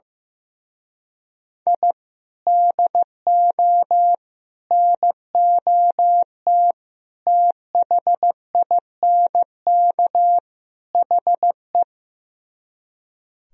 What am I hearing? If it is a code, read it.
Morse code, heard as a keyed tone, about 16 wpm, 700 Hz. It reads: I DO NOT THINK HE
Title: Speed-Racing - Sets of 5 Words 15wpm